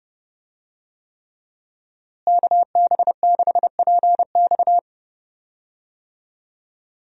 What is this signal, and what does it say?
Morse code, heard as a keyed tone, about 30 wpm, 700 Hz. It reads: KB6PX
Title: USA Call Signs - 5 Characters - Repeated 3x - 30wpm